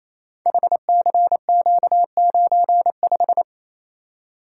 Morse code, heard as a keyed tone, about 28 wpm, 700 Hz. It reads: HCQ95
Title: Code Groups - 5 Letter:Numbers - 28wpm